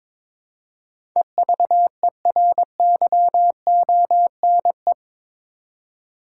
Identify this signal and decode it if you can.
Morse code, heard as a keyed tone, about 22 wpm, 700 Hz. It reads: EVERYONE
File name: practice / 22wpm / Eight Letter Words - 22wpm